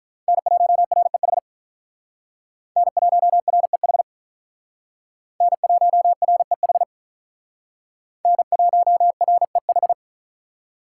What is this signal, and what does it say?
Morse code, heard as a keyed tone, about 45 wpm, 700 Hz. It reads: N1REH N1REH N1REH N1REH